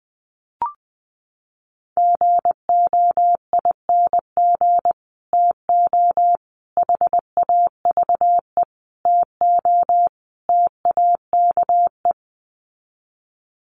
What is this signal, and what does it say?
Morse code, heard as a keyed tone, about 20 wpm, 700 Hz. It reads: GOING TO HAVE TO TAKE
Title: Sets of 5 Words Encore 20wpm